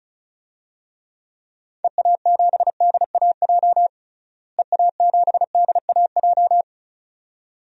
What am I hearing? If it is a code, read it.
Morse code, heard as a keyed tone, about 36 wpm, 700 Hz. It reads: EA7DAJ EA7DAJ
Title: DXCC-100 Call Signs - 6 Characters - Repeated 2x - 35wpm